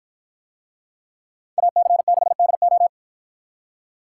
Morse code, heard as a keyed tone, about 53 wpm, 700 Hz. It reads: AC6DO